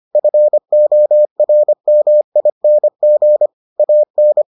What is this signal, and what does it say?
Morse code, heard as a keyed tone, about 25 wpm, 600 Hz. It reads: FORMING AN